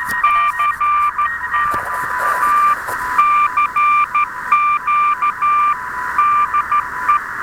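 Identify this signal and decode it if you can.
Morse code, heard as a keyed tone, about 13 wpm, 1180 Hz, with noise. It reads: CT ACQ DE
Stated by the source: Start of a QSO between UI7K and portable UA1CEG/p during the Autumn Field Day; RU-QRP Club thread on QSOs with homemade rigs.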